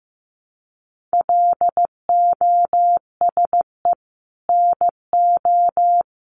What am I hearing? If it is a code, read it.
Morse code, heard as a keyed tone, about 15 wpm, 700 Hz. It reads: LOSE NO
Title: Sentences from Top 1100 Words 15wpm